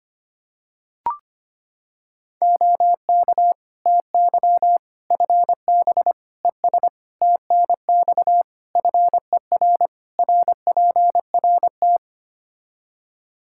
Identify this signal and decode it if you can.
Morse code, heard as a keyed tone, about 25 wpm, 700 Hz. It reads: OK TY FB ES TNX FER RPRT
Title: K1USN SST QSO Practice 25wpm